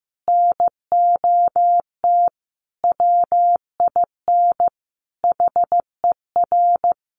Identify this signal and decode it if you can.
Morse code, heard as a keyed tone, about 15 wpm, 700 Hz. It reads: NOT WIN HER
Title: Sentences from Top 1500 Words 15wpm